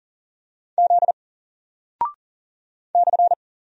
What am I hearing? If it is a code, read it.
Morse code, heard as a keyed tone, about 40 wpm, 700 Hz. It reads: Z /
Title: Single Character (Letter:Number:Punctuation:Prosign) Alt 40wpm